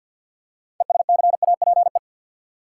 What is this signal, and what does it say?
Morse code, heard as a keyed tone, about 52 wpm, 700 Hz. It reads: ESCAPE
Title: Six Letter Words - 50wpm